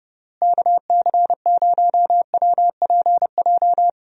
Morse code, heard as a keyed tone, about 30 wpm, 700 Hz. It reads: KC0WPJ
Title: USA Call Signs - 6 Characters - Repeated 2x - 30wpm